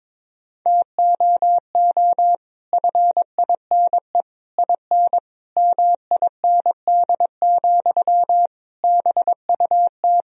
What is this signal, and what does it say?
Morse code, heard as a keyed tone, about 22 wpm, 700 Hz. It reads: TOO FINE IN MIND, BUT
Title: Sentences from Top 1000 Words - Review Part 4 - 22wpm